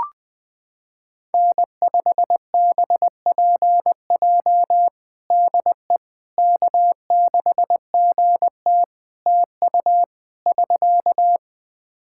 Morse code, heard as a keyed tone, about 19 wpm, 700 Hz. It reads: N5BPJ DE K6GT TU <SK>
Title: K1USN SST QSO Practice 20wpm